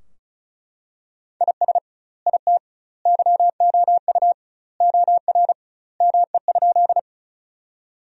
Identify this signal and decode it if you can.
Morse code, heard as a keyed tone, about 34 wpm, 700 Hz. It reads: IS IT YOU OR ME?
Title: 2x Word-Spacing - Sentences from Top 200 Words - 35wpm